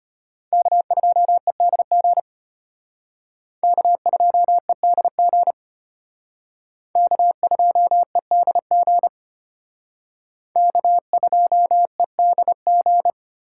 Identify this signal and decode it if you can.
Morse code, heard as a keyed tone, about 32 wpm, 700 Hz. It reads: K2EDG K2EDG K2EDG K2EDG